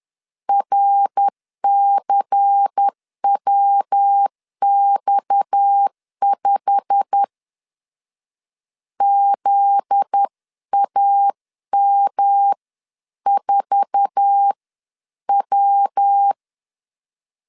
Morse code, heard as a keyed tone, about 10 wpm, spaced as for 9 wpm, 800 Hz. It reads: RCWX5 ZAM4W